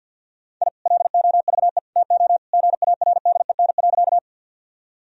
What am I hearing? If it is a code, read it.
Morse code, heard as a keyed tone, about 50 wpm, 700 Hz. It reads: I LOVE TO GARDEN.
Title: Sentences from Top 1000 Words - Review Part 4 - 50wpm